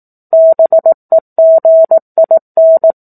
Morse code, heard as a keyed tone, about 18 wpm, 650 Hz. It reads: BEGIN
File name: 04-Chapter-04 18 WPM